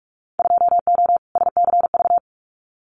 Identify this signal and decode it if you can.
Morse code, heard as a keyed tone, about 46 wpm, 700 Hz. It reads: 2O SCV
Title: Winter Field Day Exchanges Encore 45wpm